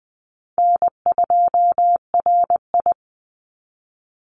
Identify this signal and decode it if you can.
Morse code, heard as a keyed tone, about 20 wpm, 700 Hz. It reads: N2RI
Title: Call Signs - 20wpm